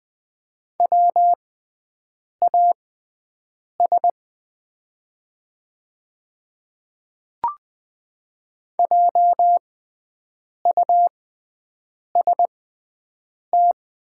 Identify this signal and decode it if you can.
Morse code, heard as a keyed tone, about 20 wpm, spaced as for 7 wpm, 690 Hz. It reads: WAS JUST
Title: 6x Character Spacing - Top 300 Words - 20wpm